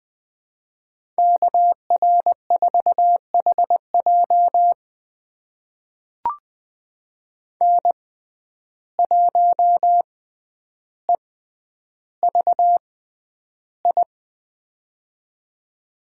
Morse code, heard as a keyed tone, about 20 wpm, spaced as for 10 wpm, 700 Hz. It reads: KR4HJ N1EVI
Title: USA Call Signs - 5 Characters - 6x Character Spacing - 20wpm